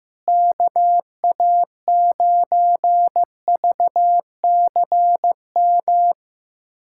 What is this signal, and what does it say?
Morse code, heard as a keyed tone, about 15 wpm, 700 Hz. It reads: KA9VCM
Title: USA Callsign DE Callsign 15wpm